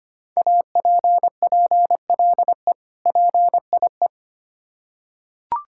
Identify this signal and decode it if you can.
Morse code, heard as a keyed tone, about 25 wpm, 700 Hz. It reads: APPLE PIE
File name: practice / 25wpm / Difficult Words with Hints - 25wpm